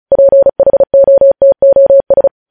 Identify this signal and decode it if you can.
Morse code, heard as a keyed tone, about 36 wpm, 550 Hz. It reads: PHOTOS